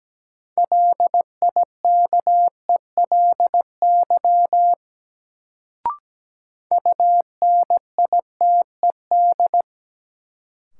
Morse code, heard as a keyed tone, about 17 wpm, 700 Hz. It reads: LIKELY UNITED
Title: Sentences from Top 600 Words - New Word Drill - 17wpm